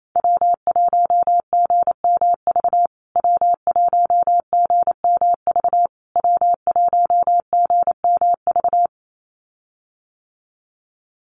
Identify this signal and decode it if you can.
Morse code, heard as a keyed tone, about 28 wpm, 700 Hz. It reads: W1GMV W1GMV W1GMV
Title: USA Call Signs Repeated 3x 28wpm